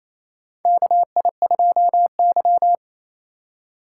Morse code, heard as a keyed tone, about 29 wpm, 700 Hz. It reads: KI2Y
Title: USA Call Signs - 4 Characters - 3x Character Spacing - 28wpm